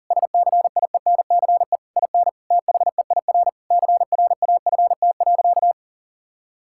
Morse code, heard as a keyed tone, about 40 wpm, 700 Hz. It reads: SCIENCE IN THEIR CRAFT.
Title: Sentences from Top 1000 Words - Review Part 6 - 40wpm